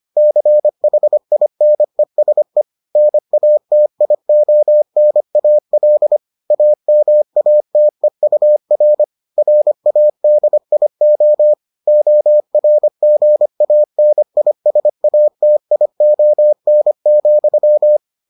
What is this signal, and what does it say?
Morse code, heard as a keyed tone, about 25 wpm, 600 Hz. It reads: CHINESE NATIONAL AMATEUR RADIO ORGANISATION,